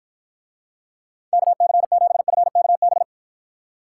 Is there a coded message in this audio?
Yes: Morse code from 1.3 to 3.0 s, heard as a keyed tone, about 53 wpm, 700 Hz. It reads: KC7FKB